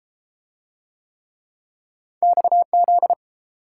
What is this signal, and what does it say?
Morse code, heard as a keyed tone, about 31 wpm, 700 Hz. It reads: XZ